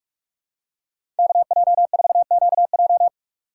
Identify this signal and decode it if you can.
Morse code, heard as a keyed tone, about 45 wpm, 700 Hz. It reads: KJ4QJ
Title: USA Call Signs - 2x Character Spacing - 45wpm